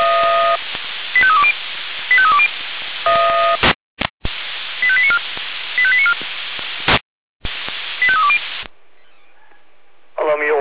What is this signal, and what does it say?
Five-tone selective call (ZVEI), heard as quick runs of five tones, tones of about 70 ms, with speech. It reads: 85320 85320 85894 85894 85320